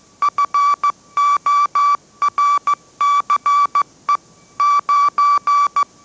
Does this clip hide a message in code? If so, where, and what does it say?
Yes: Morse code from 0.2 to 5.8 s, heard as a keyed tone, about 15 wpm, 1150 Hz. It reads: FORCE 9